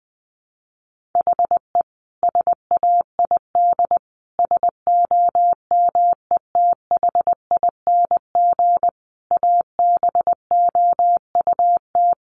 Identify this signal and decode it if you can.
Morse code, heard as a keyed tone, about 20 wpm, 700 Hz. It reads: HE SAID SOMETHING ABOUT
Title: Sentences from Top 1500 Words 20wpm